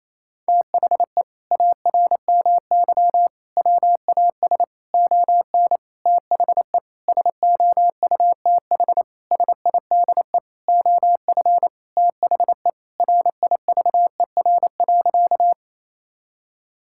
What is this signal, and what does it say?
Morse code, heard as a keyed tone, about 28 wpm, 700 Hz. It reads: THE ARMY WAS ON THE SOUTH SIDE OF THE RIVER.
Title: Sentences from Top 1000 Words - Review Part 5 - 28wpm